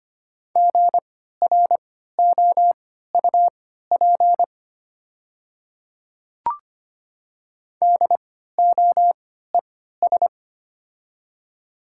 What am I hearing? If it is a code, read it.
Morse code, heard as a keyed tone, about 25 wpm, spaced as for 14 wpm, 700 Hz. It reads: GROUP DOES
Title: 3x Character Spacing - Top 300 Words - 25wpm